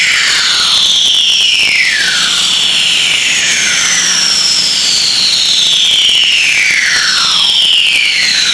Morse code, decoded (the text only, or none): none